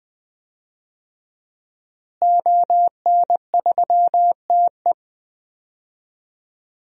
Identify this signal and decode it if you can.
Morse code, heard as a keyed tone, about 20 wpm, 700 Hz. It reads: ON3TE
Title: DXCC-100 Call Signs - 5 Characters - Repeated 2x - 20wpm